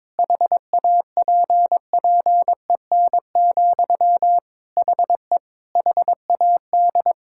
Morse code, heard as a keyed tone, about 22 wpm, 700 Hz. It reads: HAPPEN, HE HAD
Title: Sentences from Top 1000 Words - Review Part 6 - 22wpm